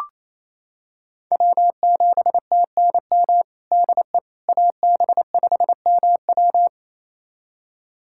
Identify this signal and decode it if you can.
Morse code, heard as a keyed tone, about 28 wpm, 690 Hz. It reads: W7TNM DE AB5MW
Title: USA Callsign DE Callsign - 5 Characters - 28wpm